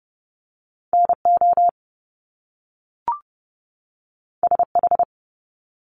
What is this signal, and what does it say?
Morse code, heard as a keyed tone, about 30 wpm, 700 Hz. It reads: NO SH
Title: Top Two Letter Combinations in Words 30wpm